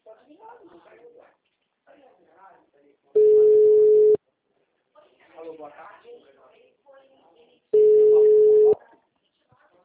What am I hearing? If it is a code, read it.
Ringback tone (Europe, 425 Hz), with speech.
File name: out- -2020- - - .35162